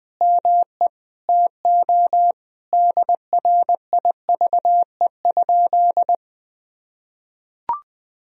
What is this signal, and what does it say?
Morse code, heard as a keyed tone, about 21 wpm, 700 Hz. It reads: ME TO DRIVE?
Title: Sentences from Top 1000 Words - Review Part 2 - 20wpm